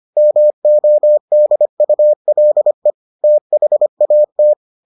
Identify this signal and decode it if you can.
Morse code, heard as a keyed tone, about 25 wpm, 600 Hz. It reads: MODULE THAT